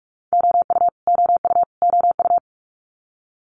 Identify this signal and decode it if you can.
Morse code, heard as a keyed tone, about 46 wpm, 700 Hz. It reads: OU OU OU
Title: Multi-letter Phonemes Repeated 3x 45wpm